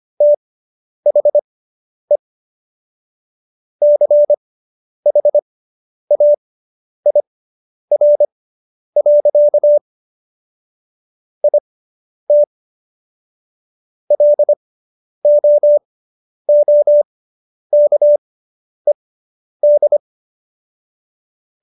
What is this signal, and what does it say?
Morse code, heard as a keyed tone, about 25 wpm, spaced as for 10 wpm, 590 Hz. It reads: THE CHAIR. IT LOOKED